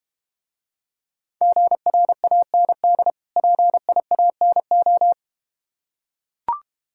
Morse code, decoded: GRAND PIANO

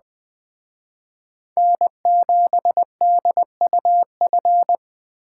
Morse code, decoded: N7DUF

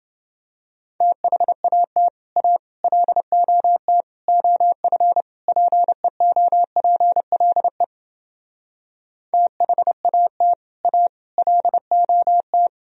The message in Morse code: THAT A LOT OF PEOPLE THAT A LOT

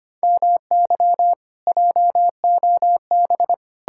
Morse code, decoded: MY JOB

